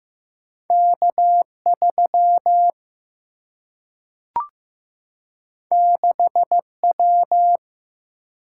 Morse code, heard as a keyed tone, about 15 wpm, 700 Hz. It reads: K3 6W
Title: Single-Letter Single-Number 15wpm